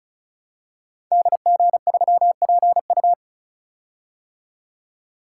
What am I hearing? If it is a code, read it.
Morse code, heard as a keyed tone, about 36 wpm, 700 Hz. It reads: DG3PU
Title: Code Groups - 5 Letter:Numbers - 35wpm